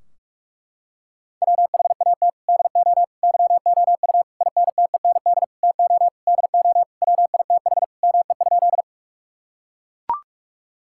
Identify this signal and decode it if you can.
Morse code, heard as a keyed tone, about 41 wpm, 700 Hz. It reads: WHAT DO YOU INTEND TO DO WITH ME?
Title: Sentences from Top 1600 Words 45wpm